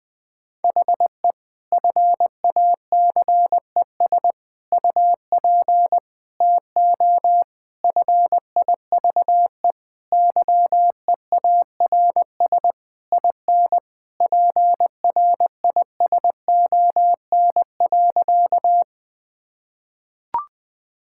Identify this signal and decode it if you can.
Morse code, heard as a keyed tone, about 20 wpm, 700 Hz. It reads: HE FACES UP TO FIVE YEARS IN PRISON.
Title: Sentences with 6-Word Phrases - 20wpm